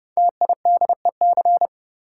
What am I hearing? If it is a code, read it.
Morse code, heard as a keyed tone, about 30 wpm, 700 Hz. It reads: TIDEC
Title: Code Groups - 5 Letters - 30wpm